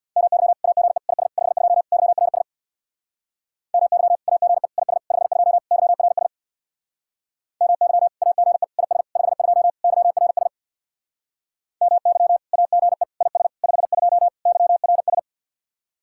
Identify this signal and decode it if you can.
Morse code, heard as a keyed tone, about 62 wpm, 700 Hz. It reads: MY AGE IS 52 YRS MY AGE IS 52 YRS MY AGE IS 52 YRS MY AGE IS 52 YRS
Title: Speed-Racing Common QSO Phrases - 50wpm